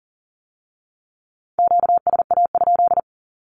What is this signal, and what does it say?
Morse code, heard as a keyed tone, about 38 wpm, 700 Hz. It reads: QSA?
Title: Q-Codes with Spoken Meaning - 40wpm